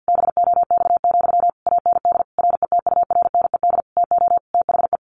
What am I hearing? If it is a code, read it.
Morse code, heard as a keyed tone, about 48 wpm, 700 Hz. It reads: BOX, AND RETURNED TO THE
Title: A Sound Of Thunder 50wpm